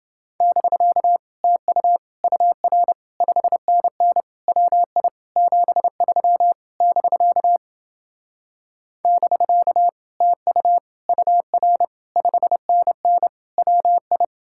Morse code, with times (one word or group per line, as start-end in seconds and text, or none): 0.4-1.2 <BK>
1.4-2.0 TU
2.2-2.9 UR
3.2-4.2 5NN
4.5-5.1 WI
5.4-6.5 73
6.8-7.6 <BK>
9.1-9.9 <BK>
10.2-10.8 TU
11.1-11.8 UR
12.2-13.3 5NN
13.6-14.2 WI